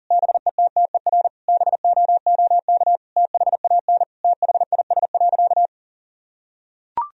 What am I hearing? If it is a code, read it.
Morse code, heard as a keyed tone, about 40 wpm, 700 Hz. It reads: BETTER BOOK THAN THIS.
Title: Sentences from Top 1000 Words - Review Part 3 - 40wpm